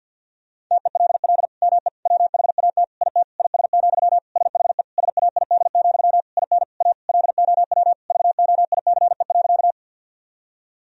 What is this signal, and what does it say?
Morse code, heard as a keyed tone, about 53 wpm, 700 Hz. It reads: TELL ME WHAT IT IS, SHE SAID, IN A LOW VOICE.
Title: Sentences from Top 1000 Words - Review Part 1 - 50wpm